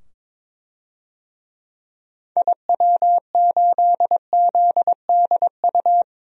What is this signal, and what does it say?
Morse code, heard as a keyed tone, about 22 wpm, 700 Hz. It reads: IW8ZDU
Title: DXCC-100 Call Signs - 6 Characters - 22wpm